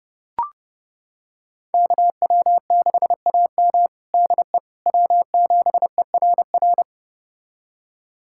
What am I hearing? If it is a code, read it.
Morse code, heard as a keyed tone, about 30 wpm, 700 Hz. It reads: KW6AM DE W7ERR
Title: USA Callsign DE Callsign - 5 Characters - 30wpm